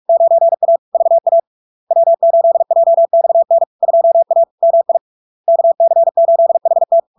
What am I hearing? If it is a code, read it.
Morse code, heard as a keyed tone, about 46 wpm, 670 Hz. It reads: 9A VA W8JXN 2A MI KC8HT
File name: LARK-FD-2022-Log-in-CW-45wpm-670-Hz